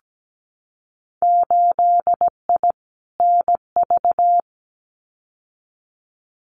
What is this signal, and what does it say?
Morse code, heard as a keyed tone, about 17 wpm, 700 Hz. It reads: 8I NV
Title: Winter Field Day Exchanges Encore 17wpm